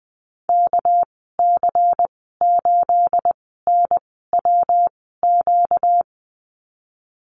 Morse code, heard as a keyed tone, about 20 wpm, spaced as for 14 wpm, 700 Hz. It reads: KC8NWQ
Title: 238 Lesson 38g - Letter q - Call Signs 20wpm